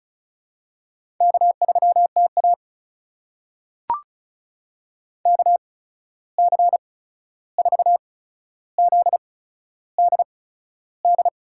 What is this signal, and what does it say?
Morse code, heard as a keyed tone, about 36 wpm, spaced as for 13 wpm, 700 Hz. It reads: K3TA KC4ZDD
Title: USA Call Signs - 8x Character Spacing - 35wpm